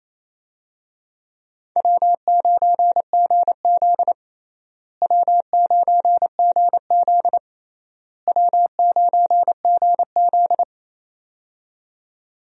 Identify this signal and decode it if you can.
Morse code, heard as a keyed tone, about 28 wpm, 700 Hz. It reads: W9GZ W9GZ W9GZ
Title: USA Call Signs - 4 Characters - Repeated 3x - 28wpm